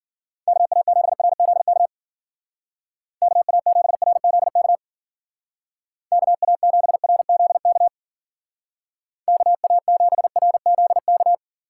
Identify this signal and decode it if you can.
Morse code, heard as a keyed tone, about 50 wpm, 700 Hz. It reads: KA7RZK KA7RZK KA7RZK KA7RZK